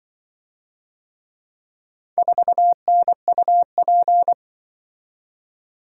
Morse code, heard as a keyed tone, about 24 wpm, 700 Hz. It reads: 4NUP
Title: Speed-Racing - Four Letter-Number Combinations - 20wpm